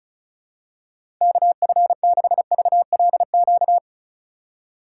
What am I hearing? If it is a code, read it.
Morse code, heard as a keyed tone, about 36 wpm, 700 Hz. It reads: KF6VLQ
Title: USA Call Signs - 6 Characters - 2x Character Spacing - 35wpm